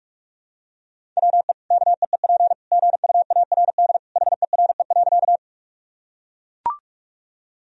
Morse code, heard as a keyed tone, about 41 wpm, 700 Hz. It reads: WE KEEP GUARD HERE.
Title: Sentences from Top 1000 Words - Review Part 4 - 45wpm